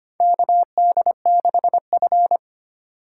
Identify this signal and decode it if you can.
Morse code, heard as a keyed tone, about 25 wpm, 700 Hz. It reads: KD6F
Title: USA Call Signs - 4 Characters - 2x Character Spacing - 25wpm